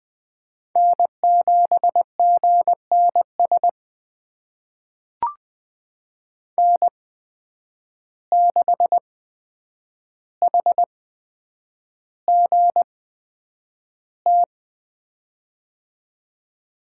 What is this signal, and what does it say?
Morse code, heard as a keyed tone, about 20 wpm, spaced as for 9 wpm, 700 Hz. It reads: N7GNS N6HGT
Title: USA Call Signs - 5 Characters - 8x Character Spacing - 20wpm